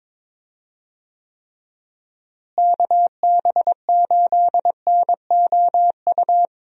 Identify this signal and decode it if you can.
Morse code, heard as a keyed tone, about 22 wpm, 700 Hz. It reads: KB8NOU
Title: USA Call Signs - 6 Characters - Repeated 2x - 22wpm